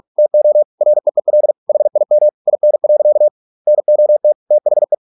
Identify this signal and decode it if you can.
Morse code, heard as a keyed tone, about 46 wpm, 600 Hz. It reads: TO REEL HIM IN. NOT THE